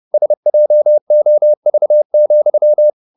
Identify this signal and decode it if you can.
Morse code, heard as a keyed tone, about 30 wpm, 600 Hz. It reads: SJOV,
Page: Ferie 30 wpm